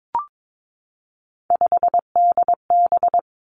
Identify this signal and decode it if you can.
Morse code, heard as a keyed tone, about 22 wpm, 700 Hz. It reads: 5DB
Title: Three Challenging Letter-Number Combinations 22wpm